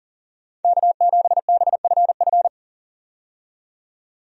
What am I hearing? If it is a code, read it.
Morse code, heard as a keyed tone, about 38 wpm, 700 Hz. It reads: K7BFF
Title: USA Call Signs - 5 Characters - 40wpm